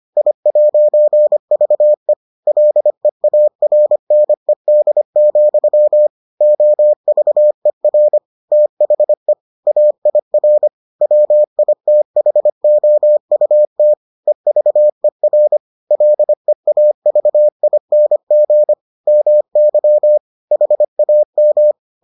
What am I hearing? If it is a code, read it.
Morse code, heard as a keyed tone, about 25 wpm, 600 Hz. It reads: I'VE LEARNED, OVER THE AIR WITHOUT EVER LEAVING MY HAM